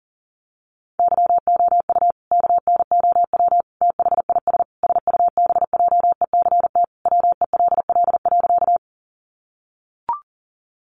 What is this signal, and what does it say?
Morse code, heard as a keyed tone, about 40 wpm, 700 Hz. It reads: YOU KNOW THIS SUBJECT WELL.
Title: Sentences from Top 800 Words 40wpm